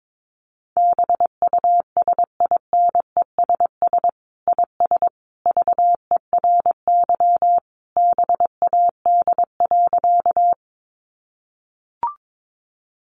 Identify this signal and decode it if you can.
Morse code, heard as a keyed tone, about 22 wpm, 700 Hz. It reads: BUSINESS IS VERY BAD.
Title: Sentences from Top 700 Words 22wpm